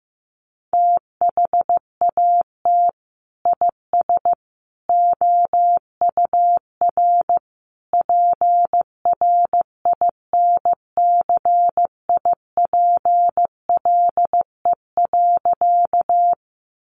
Morse code, heard as a keyed tone, about 15 wpm, 700 Hz. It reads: THAT IS OUR PRINCIPLE.
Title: Sentences from Top 1400 Words 15wpm